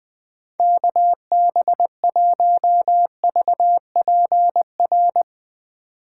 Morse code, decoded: KB1VPR

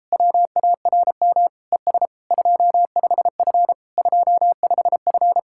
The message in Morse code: WARM ES 25F 25F

